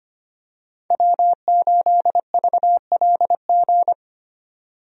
Morse code, heard as a keyed tone, about 25 wpm, 700 Hz. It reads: W8VLG
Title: USA Call Signs - 5 Characters - 25wpm